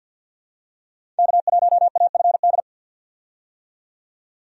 Morse code, heard as a keyed tone, about 52 wpm, 700 Hz. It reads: K1AFD
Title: USA Call Signs - 5 Characters - 50wpm